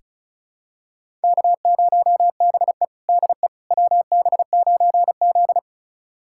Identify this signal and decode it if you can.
Morse code, heard as a keyed tone, about 34 wpm, 700 Hz. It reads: K0BE DE WB9Z
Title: USA Callsign DE Callsign - 4 Characters - 35wpm